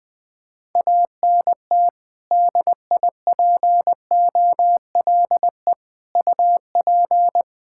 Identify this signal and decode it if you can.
Morse code, heard as a keyed tone, about 21 wpm, 700 Hz. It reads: ANT DIPOLE UP
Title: Practice QSO - Exchange 2 - Info - 20wpm